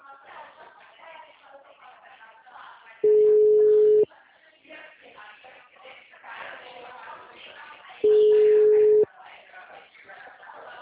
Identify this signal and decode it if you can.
Ringback tone (Europe, 425 Hz).
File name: out- -2010- - - .13695